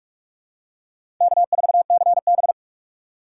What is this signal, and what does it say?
Morse code, heard as a keyed tone, about 46 wpm, 690 Hz. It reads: K4CB
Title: USA Call Signs - 4 Characters - 2x Character Spacing - 45wpm